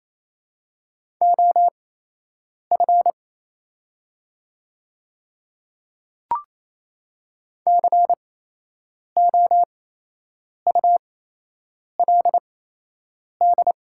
Morse code, heard as a keyed tone, about 28 wpm, spaced as for 8 wpm, 700 Hz. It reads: OF COULD
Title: 8x Character Spacing - Top 100 Words - 28wpm